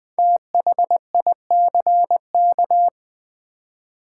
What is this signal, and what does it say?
Morse code, heard as a keyed tone, about 20 wpm, 700 Hz. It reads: THICK
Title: Five Letter Words - 20wpm